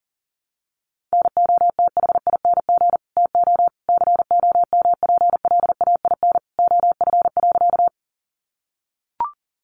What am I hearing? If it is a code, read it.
Morse code, heard as a keyed tone, about 40 wpm, 700 Hz. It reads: NOTHING TO COMPLAIN OF.
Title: Sentences from Top 1300 Words 40wpm